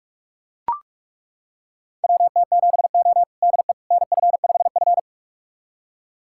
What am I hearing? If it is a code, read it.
Morse code, heard as a keyed tone, about 41 wpm, 700 Hz. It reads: WT7O DE NF5F
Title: USA Callsign DE Callsign - 4 Characters - 45wpm